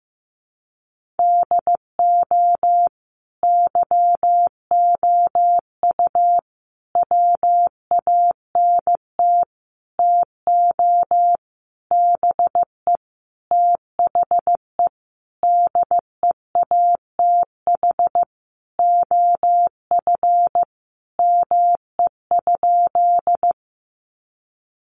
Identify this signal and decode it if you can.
Morse code, heard as a keyed tone, about 15 wpm, 700 Hz. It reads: DO YOU WANT TO BE THE DEATH OF ME?
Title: Sentences from Top 1100 Words 15wpm